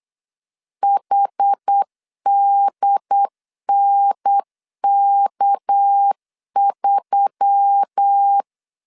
Morse code, heard as a keyed tone, about 9 wpm, 800 Hz. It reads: HDNK3
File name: CW / DET / 1 6 1500C S A 8wpm 5203